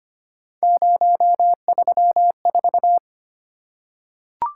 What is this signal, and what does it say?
Morse code, heard as a keyed tone, about 25 wpm, 700 Hz. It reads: 034
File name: Three Letter-Number Combinations - 25wpm